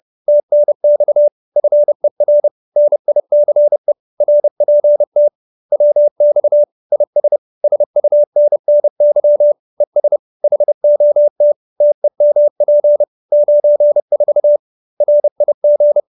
Morse code, heard as a keyed tone, about 30 wpm, 590 Hz. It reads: TNX FER NICE RPT WX IS SUNNY ES HOT TEMP 94 RIG